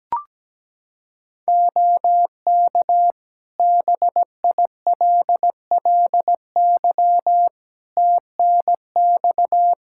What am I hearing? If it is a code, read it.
Morse code, heard as a keyed tone, about 17 wpm, 700 Hz. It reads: OK BILLY TNX
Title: Practice QSO - Exchange 4 - Ending - 17wpm